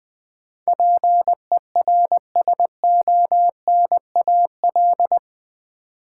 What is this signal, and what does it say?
Morse code, heard as a keyed tone, about 20 wpm, 700 Hz. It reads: PERSONAL
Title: Sentences from Top 700 Words - New Word Drill - 20wpm